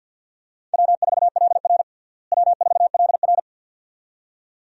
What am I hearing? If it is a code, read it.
Morse code, heard as a keyed tone, about 52 wpm, 700 Hz. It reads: W4LR W4LR
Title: USA Call Signs - 4 Characters - Repeated 2x - 50wpm